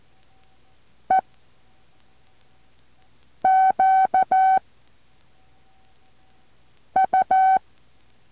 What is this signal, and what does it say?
Morse code, heard as a keyed tone, about 14 wpm, 760 Hz. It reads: E Q U
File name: intl morse code2